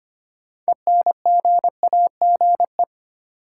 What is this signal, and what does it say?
Morse code, heard as a keyed tone, about 25 wpm, 700 Hz. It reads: ENGAGE
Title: Sentences from Top 1500 Words - New Word Drill Alt - 25wpm